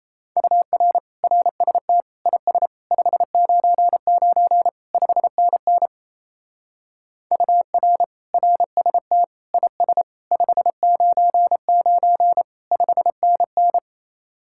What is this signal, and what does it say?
Morse code, heard as a keyed tone, about 36 wpm, 700 Hz. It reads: UR RST IS 599 5NN UR RST IS 599 5NN